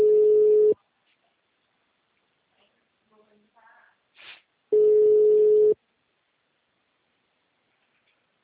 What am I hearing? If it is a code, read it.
Ringback tone (Europe, 425 Hz), with speech.